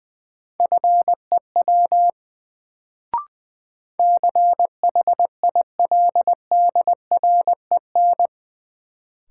Morse code, heard as a keyed tone, about 21 wpm, 700 Hz. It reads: FEW CHILDREN
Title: Sentences from Top 300 Words - New Word Drill - 20wpm